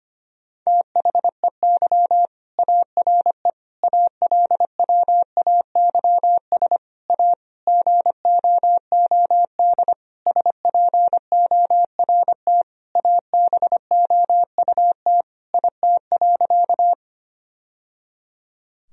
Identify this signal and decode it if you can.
Morse code, heard as a keyed tone, about 25 wpm, 700 Hz. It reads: THEY ARE ALWAYS A GOOD SPORT ABOUT IT.
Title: Sentences from Top 1000 Words - Review Part 4 - 25wpm